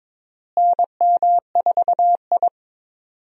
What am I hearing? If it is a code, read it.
Morse code, heard as a keyed tone, about 21 wpm, 700 Hz. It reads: NM4I
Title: USA Call Signs - 4 Characters - Repeated 3x - 22wpm